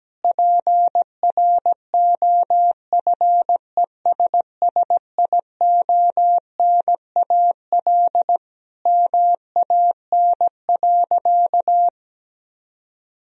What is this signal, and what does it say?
Morse code, heard as a keyed tone, about 17 wpm, 700 Hz. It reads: PROFESSIONAL MAN.
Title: Sentences from Top 1000 Words - Review Part 4 - 17wpm